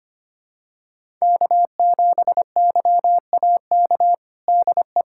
Morse code, heard as a keyed tone, about 25 wpm, 700 Hz. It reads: K7YAK DE
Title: USA Callsign DE Callsign - 5 Characters - 25wpm